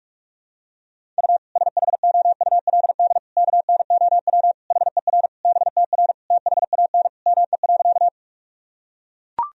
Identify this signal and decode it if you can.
Morse code, heard as a keyed tone, about 41 wpm, 700 Hz. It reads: U SHOULD KNOW HER BTR THAN ME.
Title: Sentences from Top 400 Words and Abbreviations 45wpm